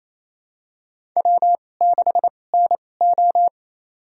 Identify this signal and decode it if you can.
Morse code, heard as a keyed tone, about 27 wpm, spaced as for 20 wpm, 700 Hz. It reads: W6NO
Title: USA Call Signs - 4 Characters - 2x Character Spacing - 28wpm